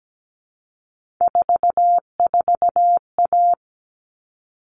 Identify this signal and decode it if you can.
Morse code, heard as a keyed tone, about 17 wpm, 700 Hz. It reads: 44A
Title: All Cut Numbers - 3-Digit 17wpm